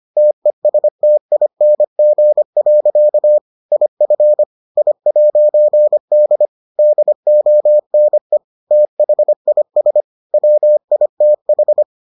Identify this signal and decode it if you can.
Morse code, heard as a keyed tone, about 25 wpm, 600 Hz. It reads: TESTING. IF I'D DONE THIS WITH